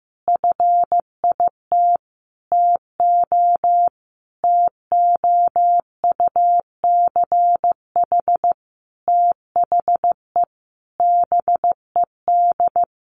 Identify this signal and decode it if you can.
Morse code, heard as a keyed tone, about 15 wpm, 700 Hz. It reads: FIT TO TOUCH THE BED
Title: Sentences from Top 1200 Words 15wpm